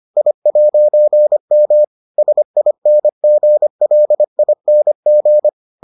Morse code, heard as a keyed tone, about 25 wpm, 600 Hz. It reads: I'M SINGLING